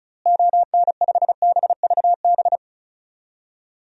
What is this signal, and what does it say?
Morse code, heard as a keyed tone, about 36 wpm, 700 Hz. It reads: ON5BVB
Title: DXCC-100 Call Signs - 6 Characters - Repeated 3x - 35wpm